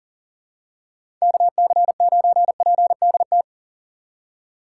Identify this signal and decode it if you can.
Morse code, heard as a keyed tone, about 40 wpm, 700 Hz. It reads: KC9PDT
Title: USA Call Signs - 6 Characters - 3x Character Spacing - 40wpm